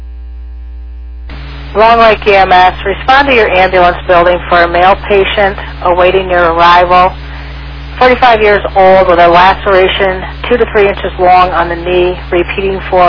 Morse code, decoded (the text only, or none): none